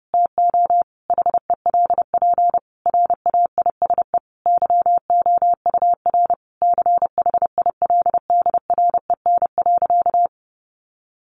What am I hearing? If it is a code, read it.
Morse code, heard as a keyed tone, about 30 wpm, 700 Hz. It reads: TO HELP RAISE YOUR CHILDREN.
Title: Sentences from Top 1400 Words 30wpm